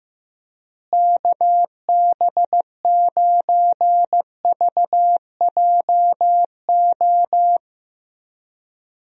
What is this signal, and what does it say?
Morse code, heard as a keyed tone, about 14 wpm, 700 Hz. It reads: KB9VJO
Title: USA Call Signs - 6 Characters - Repeated 3x - 15wpm